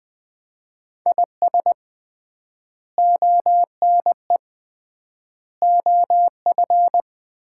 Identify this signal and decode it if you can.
Morse code, heard as a keyed tone, about 20 wpm, 700 Hz. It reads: IS ONE OF
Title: 3x Word-Spacing - Sets of 3 Words - 20wpm